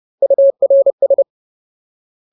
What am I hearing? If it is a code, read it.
Morse code, heard as a keyed tone, about 30 wpm, 550 Hz. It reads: URS
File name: Ultimate Word List / 30WPM Ultimate